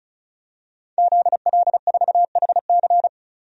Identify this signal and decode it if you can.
Morse code, heard as a keyed tone, about 35 wpm, 700 Hz. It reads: ZL4HC
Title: DXCC-100 Call Signs - 5 Characters - 35wpm